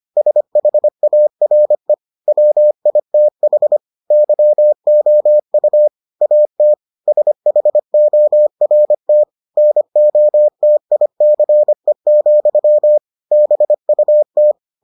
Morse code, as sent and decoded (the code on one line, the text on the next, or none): ... .... .- .-. / .-- .. - .... / -.-- --- ..- / .- - / ... .... --- .-. - / -. --- - .. -.-. --..-- / -... ..- -
SHARE WITH YOU AT SHORT NOTICE, BUT